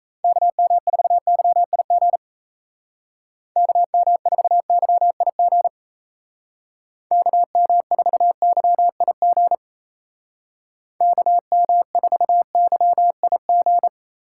Morse code, heard as a keyed tone, about 38 wpm, 700 Hz. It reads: KM4YIG KM4YIG KM4YIG KM4YIG